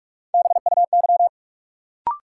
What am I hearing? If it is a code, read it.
Morse code, heard as a keyed tone, about 42 wpm, 700 Hz. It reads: BUY
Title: Sentences from Top 700 Words - New Word Drill - 45wpm